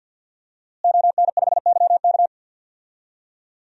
Morse code, heard as a keyed tone, about 47 wpm, 700 Hz. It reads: ON5YK
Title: DXCC-100 Call Signs - 5 Characters - Repeated 2x - 50wpm